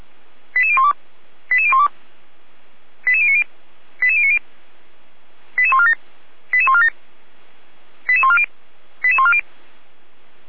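Five-tone selective call (ZVEI), heard as quick runs of five tones, tones of about 70 ms. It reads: 88012 88012 88090 88090 88147 88147 88140 88140
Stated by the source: Five-tone ZVEI selective call 88399, received by SDR, fire brigade Orlamünde.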